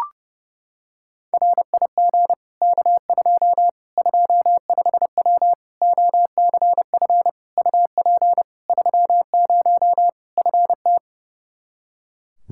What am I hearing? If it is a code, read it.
Morse code, heard as a keyed tone, about 30 wpm, 700 Hz. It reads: RIG K2 25W OCF UP 30 FT
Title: Practice QSO - Exchange 2 - Info - 30wpm